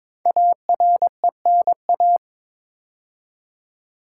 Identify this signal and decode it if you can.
Morse code, heard as a keyed tone, about 22 wpm, 700 Hz. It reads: ARENA
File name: 5 Letter Words - First 3 Letters Given - 22wpm